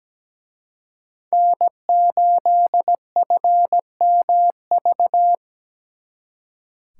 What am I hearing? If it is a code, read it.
Morse code, heard as a keyed tone, about 20 wpm, 700 Hz. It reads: N8FMV
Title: Speed Racing - USA 1x3 Call Signs 17wpm